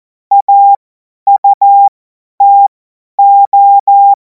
Morse code, heard as a keyed tone, about 14 wpm, spaced as for 10 wpm, 800 Hz. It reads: AUTO
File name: Morsekurs / Witze / Witz 02. Juni 2026 50 F70